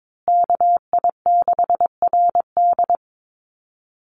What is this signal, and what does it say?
Morse code, heard as a keyed tone, about 22 wpm, 700 Hz. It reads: KI6RD
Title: USA Call Signs Repeated 2x 22wpm